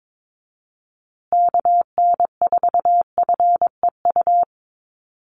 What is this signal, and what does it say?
Morse code, heard as a keyed tone, about 22 wpm, 700 Hz. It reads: KN4FEU
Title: USA Call Signs Repeated 2x 22wpm